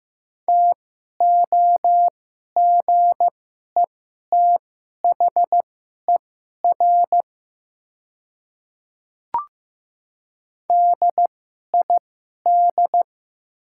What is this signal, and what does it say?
Morse code, heard as a keyed tone, about 15 wpm, spaced as for 11 wpm, 700 Hz. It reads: TOGETHER DID
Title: 2x Character Spacing - Top 300 Words - 15wpm